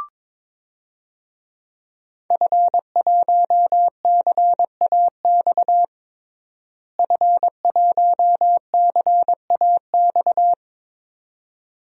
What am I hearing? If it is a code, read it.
Morse code, heard as a keyed tone, about 22 wpm, 700 Hz. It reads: F1CAX F1CAX
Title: DXCC-100 Call Signs - 5 Characters - Repeated 2x - 22wpm